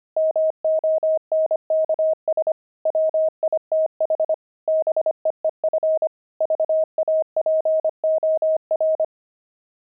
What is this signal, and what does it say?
Morse code, heard as a keyed tone, about 25 wpm, 630 Hz. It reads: MONKS WITH BEEF VAPOR
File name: pangram-fs-25-cs-25-monks